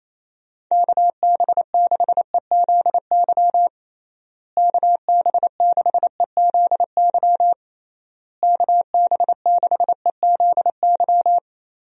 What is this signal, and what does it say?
Morse code, heard as a keyed tone, about 28 wpm, 700 Hz. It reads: KB6EZY KB6EZY KB6EZY
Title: USA Call Signs - 6 Characters - Repeated 3x - 28wpm